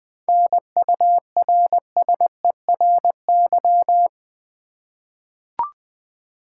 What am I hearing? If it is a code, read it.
Morse code, heard as a keyed tone, about 20 wpm, 700 Hz. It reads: NURSERY
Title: 7 Letter Words - First 3 Letters Given - 20wpm